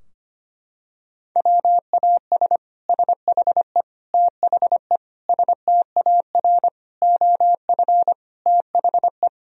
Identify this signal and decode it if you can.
Morse code, heard as a keyed tone, about 25 wpm, 700 Hz. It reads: WAS SHE THE STAR OF THE
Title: Sentences from Top 1000 Words - Review Part 3 - 25wpm